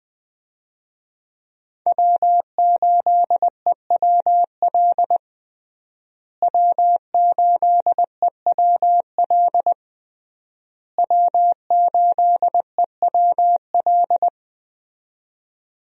Morse code, heard as a keyed tone, about 20 wpm, 700 Hz. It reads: W8EWL W8EWL W8EWL
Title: USA Call Signs - 5 Characters - Repeated 3x - 20wpm